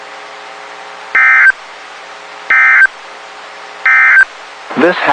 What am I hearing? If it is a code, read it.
Emergency Alert System (SAME) tones, heard as harsh buzzing bursts. It is heard from 1.0 to 4.3 s.
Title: WXK91 NOAA Weather Radio